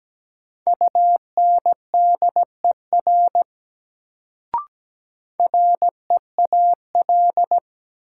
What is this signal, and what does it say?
Morse code, heard as a keyed tone, about 17 wpm, 700 Hz. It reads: UNDER REAL
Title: Sentences from Top 300 Words - New Word Drill - 17wpm